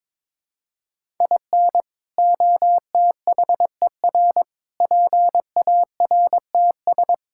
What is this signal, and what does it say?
Morse code, heard as a keyed tone, about 22 wpm, 700 Hz. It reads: IN OTHER PARTS
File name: practice / 22wpm / Sets of 6 Words - 22wpm